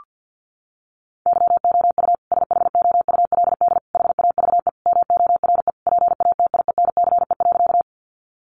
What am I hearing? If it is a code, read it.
Morse code, heard as a keyed tone, about 50 wpm, 700 Hz. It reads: YOU SHOULD HAVE MORE PATIENCE.
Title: Sentences from Top 1300 Words 50wpm